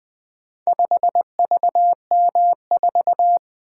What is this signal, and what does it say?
Morse code, heard as a keyed tone, about 20 wpm, 700 Hz. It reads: 5VM4
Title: Four Letter-Number Combinations - 20wpm